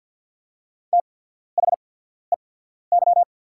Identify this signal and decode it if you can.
Morse code, heard as a keyed tone, about 48 wpm, spaced as for 14 wpm, 690 Hz. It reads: THEY
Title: 8x Character Spacing - Top 500 Words - 50wpm